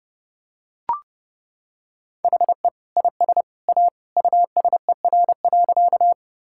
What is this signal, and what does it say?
Morse code, heard as a keyed tone, about 30 wpm, 700 Hz. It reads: HE IS A USER.
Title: Sentences from Top 1000 Words - Review Part 5 - 30wpm